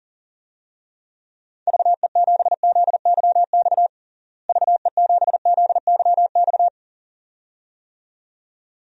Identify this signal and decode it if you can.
Morse code, heard as a keyed tone, about 37 wpm, 700 Hz. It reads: VE7ZYX VE7ZYX
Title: DXCC-100 Call Signs - 6 Characters - Repeated 2x - 40wpm